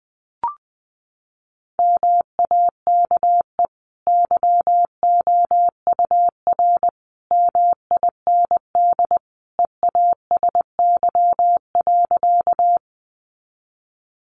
Morse code, heard as a keyed tone, about 20 wpm, 700 Hz. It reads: MAKE YOUR MIND EASY.
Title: Sentences from Top 700 Words 20wpm